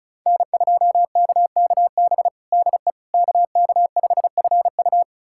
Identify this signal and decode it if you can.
Morse code, heard as a keyed tone, about 36 wpm, 700 Hz. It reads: N2KKB DE KK5FU
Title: USA Callsign DE Callsign 35wpm